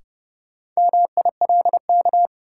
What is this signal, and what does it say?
Morse code, heard as a keyed tone, about 31 wpm, 700 Hz. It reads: MILK
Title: Sentences from Top 1500 Words - New Word Drill Alt - 30wpm